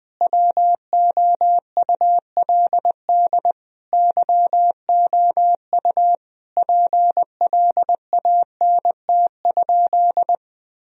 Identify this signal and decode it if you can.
Morse code, heard as a keyed tone, about 21 wpm, 700 Hz. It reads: WOULD YOU PLANT?
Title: Sentences from Top 1000 Words - Review Part 1 - 20wpm